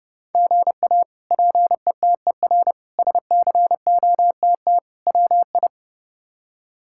Morse code, heard as a keyed tone, about 31 wpm, 700 Hz. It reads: GA PETER SCOTT WI
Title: K1USN SST QSO Practice - 30wpm